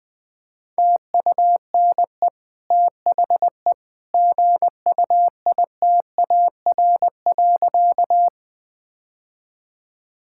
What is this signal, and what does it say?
Morse code, heard as a keyed tone, about 20 wpm, 700 Hz. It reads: TUNE THE GUITAR.
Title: Easy Sentences with Hints - 20wpm